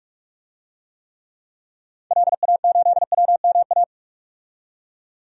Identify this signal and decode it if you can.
Morse code, heard as a keyed tone, about 45 wpm, 700 Hz. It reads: LA8WMA
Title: DXCC-100 Call Signs - 6 Characters - 45wpm